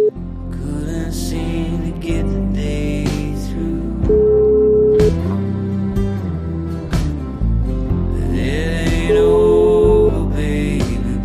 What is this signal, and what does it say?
Ringback tone (Europe, 425 Hz), with noise.